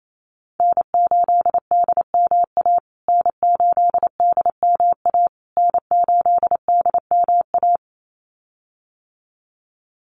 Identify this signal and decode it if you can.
Morse code, heard as a keyed tone, about 28 wpm, 700 Hz. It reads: N8DMA N8DMA N8DMA
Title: USA Call Signs Repeated 3x 28wpm